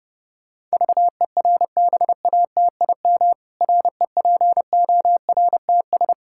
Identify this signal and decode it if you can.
Morse code, heard as a keyed tone, about 30 wpm, 700 Hz. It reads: VERBATIM REPORTS
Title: Pangram Challenge v2 - 30wpm